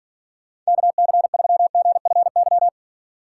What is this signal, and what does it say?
Morse code, heard as a keyed tone, about 49 wpm, 700 Hz. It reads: KC3GFY
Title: Speed Racing - USA 2x3 Call Signs 40wpm